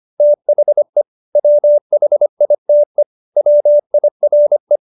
Morse code, heard as a keyed tone, about 25 wpm, 590 Hz. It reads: THE WHITE WIRE